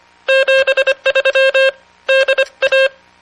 Morse code, heard as a keyed tone, about 24 wpm, 540 Hz. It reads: 73 DA